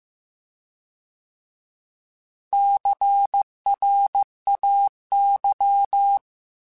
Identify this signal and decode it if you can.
Morse code, heard as a keyed tone, about 15 wpm, 800 Hz. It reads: CRAY